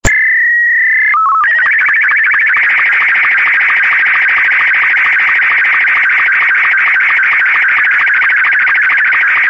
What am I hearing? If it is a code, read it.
SSTV (slow-scan television), Martin 2.
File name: sstv / sstv2 / martin m2.48k